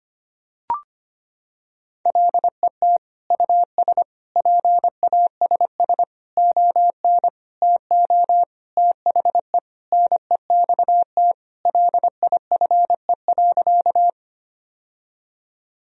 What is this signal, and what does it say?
Morse code, heard as a keyed tone, about 25 wpm, 700 Hz. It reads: LET US PASS ON TO THE NEXT LIFE.
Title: Sentences from Top 1000 Words - Review Part 1 - 25wpm